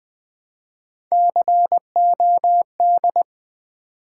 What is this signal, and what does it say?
Morse code, heard as a keyed tone, about 20 wpm, 700 Hz. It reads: COD
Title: Build A Word - 1 Letter At A Time - 4-Letter Words - 20wpm